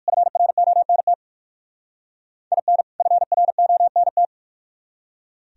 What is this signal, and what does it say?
Morse code, heard as a keyed tone, about 52 wpm, 700 Hz. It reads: FRONT IN FRONT
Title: Speed-Racing - Sets of 2 Words 45wpm